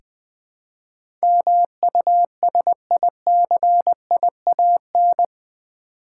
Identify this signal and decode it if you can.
Morse code, heard as a keyed tone, about 20 wpm, 700 Hz. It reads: MUSICIAN
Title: Build A Word - 2 Letters At A Time - 8-Letter Words - 20wpm